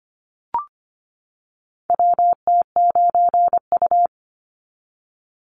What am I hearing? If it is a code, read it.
Morse code, heard as a keyed tone, about 25 wpm, 700 Hz. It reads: WT9U
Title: Call Signs - 25wpm